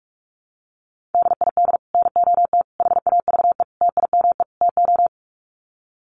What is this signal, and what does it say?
Morse code, heard as a keyed tone, about 44 wpm, 700 Hz. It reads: DID NOT HAVE TIME TO
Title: Sets of 5 Words Encore 45wpm